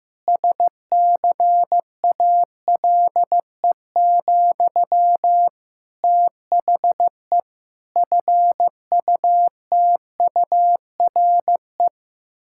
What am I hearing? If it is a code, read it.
Morse code, heard as a keyed tone, about 16 wpm, 700 Hz. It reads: SCALE, THE FUTURE
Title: Sentences from Top 1000 Words - Review Part 5 - 15wpm